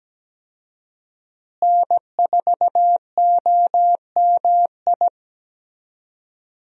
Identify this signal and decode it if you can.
Morse code, heard as a keyed tone, about 17 wpm, 700 Hz. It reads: N4OMI
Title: USA Call Signs - 5 Characters - Repeated 2x - 17wpm